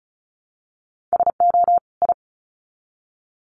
Morse code, heard as a keyed tone, about 34 wpm, 700 Hz. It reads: SO I
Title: Sets of 2 Words 35wpm